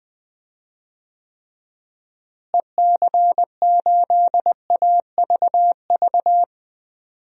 Morse code, heard as a keyed tone, about 19 wpm, 700 Hz. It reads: EC8AVV
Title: DXCC-100 Call Signs - 6 Characters - Repeated 3x - 20wpm